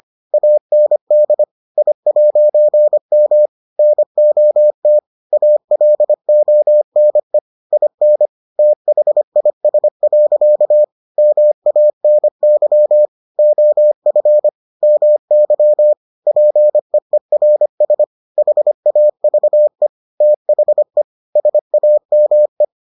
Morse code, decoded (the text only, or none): AND I'M NOT ALONE IN THIS. MANY OF MY PEERS HAVE THE SAME